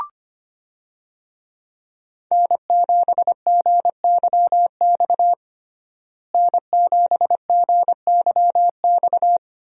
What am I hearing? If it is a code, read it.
Morse code, heard as a keyed tone, about 25 wpm, 700 Hz. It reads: N7GYX N7GYX
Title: USA Call Signs - 5 Characters - Repeated 3x - 25wpm